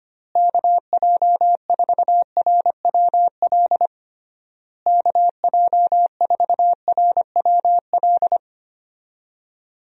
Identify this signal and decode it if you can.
Morse code, heard as a keyed tone, about 25 wpm, 700 Hz. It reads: KJ4RWL KJ4RWL
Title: USA Call Signs - 6 Characters - Repeated 3x - 25wpm